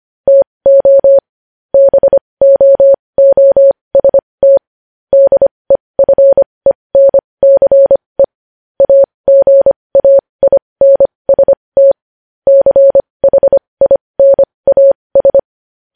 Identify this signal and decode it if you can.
Morse code, heard as a keyed tone, about 25 wpm, spaced as for 20 wpm, 550 Hz. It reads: TO BOOST DEFENCE AGAINST CHINAS
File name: podcast / News headlines-1x-25x20